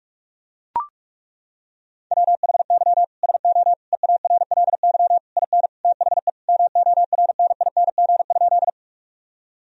Morse code, heard as a keyed tone, about 41 wpm, 700 Hz. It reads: WHY SO EARLY IN THE MORNING?
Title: Sentences from Top 1000 Words - Review Part 2 - 45wpm